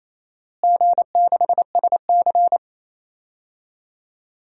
Morse code, heard as a keyed tone, about 28 wpm, 700 Hz. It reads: G6SC
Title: DXCC-100 Call Signs - 4 Characters - Repeated 3x - 28wpm